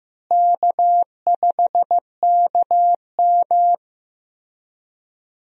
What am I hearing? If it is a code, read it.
Morse code, heard as a keyed tone, about 15 wpm, 700 Hz. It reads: K5KM
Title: USA Call Signs - 4 Characters - Repeated 3x - 15wpm